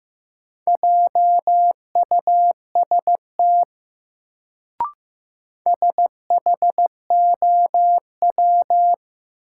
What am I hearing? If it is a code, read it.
Morse code, heard as a keyed tone, about 15 wpm, 700 Hz. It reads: JUST SHOW
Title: Sentences from Top 200 Words - New Word Drill - 15wpm